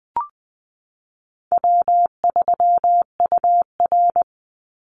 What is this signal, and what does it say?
Morse code, heard as a keyed tone, about 20 wpm, 700 Hz. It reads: W3UR
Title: Easy Call Signs Repeated 2x - 20wpm